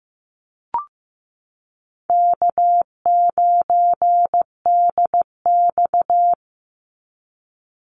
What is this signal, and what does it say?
Morse code, heard as a keyed tone, about 15 wpm, 700 Hz. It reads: K9DX
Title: Easy Call Signs - 15wpm